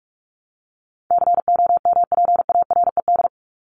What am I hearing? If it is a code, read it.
Morse code, heard as a keyed tone, about 44 wpm, 700 Hz. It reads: COMPARED
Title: ED Suffix - Top 100 Words with Base Words 45wpm